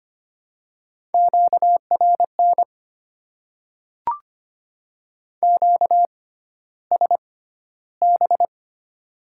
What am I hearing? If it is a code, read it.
Morse code, heard as a keyed tone, about 25 wpm, spaced as for 13 wpm, 700 Hz. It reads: QRN QSB
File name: Top 90 QSO Elements - 6x Character Spacing - 25wpm